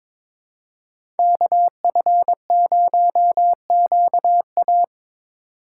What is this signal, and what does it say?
Morse code, heard as a keyed tone, about 22 wpm, 700 Hz. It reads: KF0QA
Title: USA Call Signs - 6x Character Spacing - 22wpm